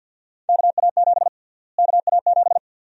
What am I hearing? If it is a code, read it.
Morse code, heard as a keyed tone, about 49 wpm, 700 Hz. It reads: KA7 KA7
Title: USA Call Sign Prefixes - Repeated 2x - 50wpm